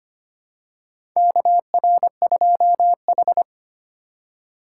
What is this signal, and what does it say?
Morse code, heard as a keyed tone, about 25 wpm, 700 Hz. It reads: KR2H
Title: USA Call Signs - 4 Characters - 3x Character Spacing - 25wpm